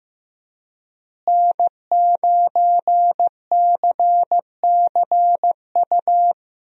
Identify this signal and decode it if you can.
Morse code, heard as a keyed tone, about 16 wpm, 700 Hz. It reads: N9CCU
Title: USA Call Signs - 5 Characters - Repeated 3x - 15wpm